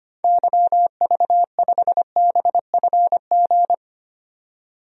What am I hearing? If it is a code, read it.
Morse code, heard as a keyed tone, about 25 wpm, 700 Hz. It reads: YV5BFG
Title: DXCC-100 Call Signs - 6 Characters - 25wpm